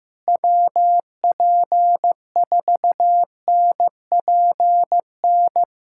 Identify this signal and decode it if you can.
Morse code, heard as a keyed tone, about 15 wpm, 700 Hz. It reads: WP4NPN
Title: USA Callsign DE Callsign 15wpm